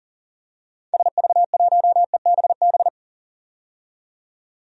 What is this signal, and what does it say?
Morse code, heard as a keyed tone, about 40 wpm, 700 Hz. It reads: SV1EBB
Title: DXCC-100 Call Signs - 6 Characters - 40wpm